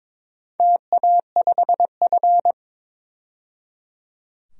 Morse code, heard as a keyed tone, about 22 wpm, 690 Hz. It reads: TA5F
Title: DXCC-100 Call Signs - 4 Characters - 22wpm